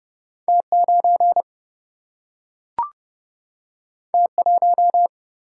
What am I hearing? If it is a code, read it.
Morse code, heard as a keyed tone, about 30 wpm, 700 Hz. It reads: T9 T1
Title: Single-Letter Single-Number 30wpm